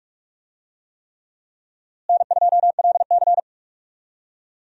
Morse code, heard as a keyed tone, about 46 wpm, 700 Hz. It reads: N2LC
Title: USA Call Signs - 4 Characters - Repeated 3x - 45wpm